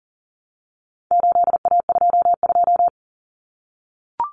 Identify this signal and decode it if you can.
Morse code, heard as a keyed tone, about 40 wpm, 700 Hz. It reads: 8A22
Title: Common Cut Numbers - 4-Digit 40wpm